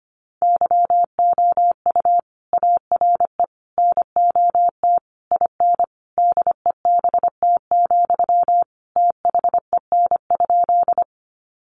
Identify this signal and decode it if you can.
Morse code, heard as a keyed tone, about 25 wpm, 700 Hz. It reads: YOU ARE NOT IN DEBT, THEN?
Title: Sentences from Top 1300 Words 25wpm